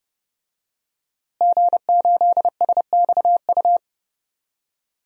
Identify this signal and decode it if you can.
Morse code, heard as a keyed tone, about 30 wpm, 700 Hz. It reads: G8SXU
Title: DXCC-100 Call Signs - 5 Characters - Repeated 2x - 30wpm